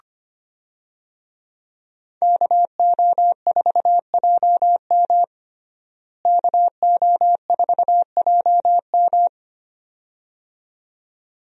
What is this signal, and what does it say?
Morse code, heard as a keyed tone, about 25 wpm, 700 Hz. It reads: KO4JM KO4JM
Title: USA Call Signs - 5 Characters - Repeated 2x - 25wpm